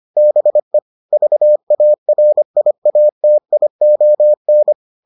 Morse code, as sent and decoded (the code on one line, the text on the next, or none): -... / ...- .- .-. .. .- - .. --- -.
BE VARIATION